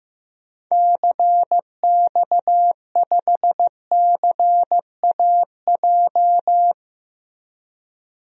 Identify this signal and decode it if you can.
Morse code, heard as a keyed tone, about 15 wpm, 700 Hz. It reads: CX5CAJ
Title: DXCC-100 Call Signs - 6 Characters - Repeated 2x - 15wpm